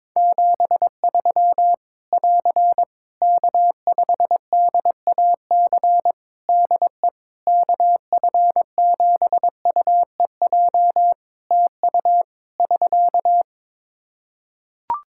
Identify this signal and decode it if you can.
Morse code, heard as a keyed tone, about 22 wpm, 700 Hz. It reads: 73 <AR> K5DAC DE KF7UEJ TU <SK>
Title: Practice QSO - Exchange 4 - Ending - 22wpm